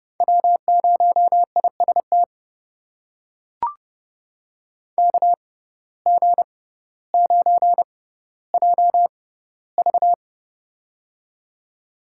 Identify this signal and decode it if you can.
Morse code, heard as a keyed tone, about 30 wpm, spaced as for 15 wpm, 690 Hz. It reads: W0IST KG9JV
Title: USA Call Signs - 5 Characters - 6x Character Spacing - 30wpm